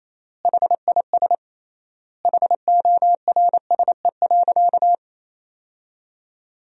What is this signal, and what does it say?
Morse code, heard as a keyed tone, about 28 wpm, 700 Hz. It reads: HIS HORSE.
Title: 3x Word-Spacing - Sentences from Top 300 Words - 28wpm